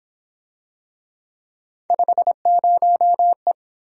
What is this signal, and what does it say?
Morse code, heard as a keyed tone, about 27 wpm, 700 Hz. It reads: 50E
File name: Speed-Racing - Three Letter-Number Combinations - 22wpm